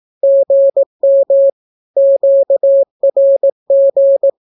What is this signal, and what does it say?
Morse code, heard as a keyed tone, about 18 wpm, 550 Hz. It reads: GM QRG
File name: naukapodstaw / skroty / skroty